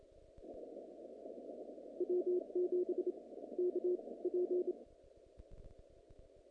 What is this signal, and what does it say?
Morse code, heard as a keyed tone, about 26 wpm, spaced as for 18 wpm, 360 Hz. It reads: W7KP
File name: contests / naqp-cw-k5tr-op-k5pi- -2200